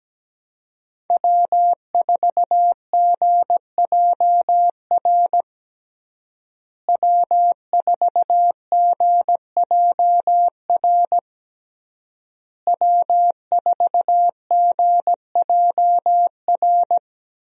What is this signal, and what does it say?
Morse code, heard as a keyed tone, about 17 wpm, 700 Hz. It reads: W4GJR W4GJR W4GJR
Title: USA Call Signs - 5 Characters - Repeated 3x - 17wpm